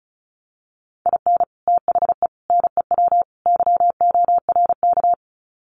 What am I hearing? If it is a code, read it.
Morse code, heard as a keyed tone, about 34 wpm, 700 Hz. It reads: IN THE NEW YORK
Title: Sets of 4 Words 35wpm